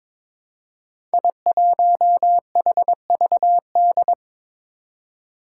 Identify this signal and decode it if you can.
Morse code, heard as a keyed tone, about 22 wpm, 700 Hz. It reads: I1HVD
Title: Five Challenging Letter-Number Combinations 22wpm